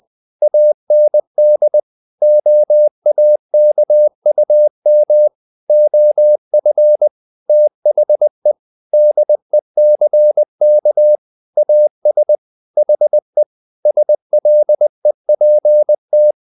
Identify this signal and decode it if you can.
Morse code, heard as a keyed tone, about 20 wpm, 600 Hz. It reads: AND OAKUM OF THE DECK AS HE SLEPT
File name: the old man and the sea / the old man and the sea part 1 20wpm